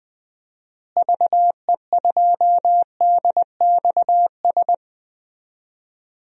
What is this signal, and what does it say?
Morse code, heard as a keyed tone, about 21 wpm, 700 Hz. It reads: VE2DXS
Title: DXCC-100 Call Signs - 6 Characters - Repeated 3x - 20wpm